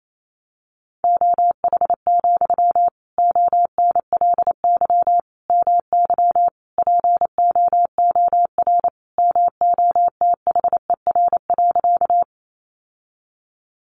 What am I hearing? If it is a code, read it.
Morse code, heard as a keyed tone, about 28 wpm, 700 Hz. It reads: OH, ONLY MY POOR MOTHER.
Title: Sentences from Top 1000 Words 28wpm